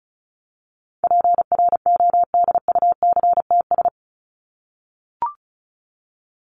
Morse code, heard as a keyed tone, about 34 wpm, 700 Hz. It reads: PRODUCTS
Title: ES:S Suffix - Top 100 Words with Base Words 35wpm